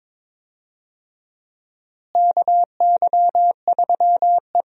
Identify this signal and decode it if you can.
Morse code, heard as a keyed tone, about 23 wpm, 700 Hz. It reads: KY3E
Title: USA Call Signs - 4 Characters - Repeated 2x - 22wpm